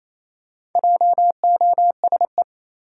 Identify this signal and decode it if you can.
Morse code, heard as a keyed tone, about 28 wpm, 700 Hz. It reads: JOSE
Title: USA Common Names - Max 4 Letters - 28wpm